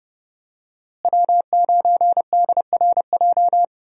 Morse code, heard as a keyed tone, about 30 wpm, 700 Hz. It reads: W9DRJ
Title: USA Call Signs - 5 Characters - Repeated 3x - 30wpm